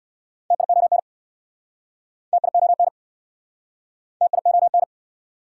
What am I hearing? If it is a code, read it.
Morse code, heard as a keyed tone, about 69 wpm, 700 Hz. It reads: TION TION TION